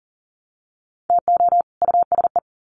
Morse code, heard as a keyed tone, about 40 wpm, 700 Hz. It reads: TO USE
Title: Sets of 2 Words 40wpm